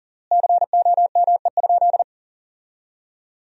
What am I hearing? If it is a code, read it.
Morse code, heard as a keyed tone, about 40 wpm, 700 Hz. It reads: COME?
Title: 3x Word-Spacing - Sentences from Top 200 Words - 40wpm